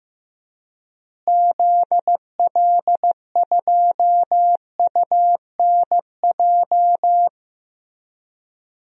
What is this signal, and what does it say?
Morse code, heard as a keyed tone, about 15 wpm, 700 Hz. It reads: ZL2UNJ
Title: DXCC-100 Call Signs - 6 Characters - Repeated 3x - 15wpm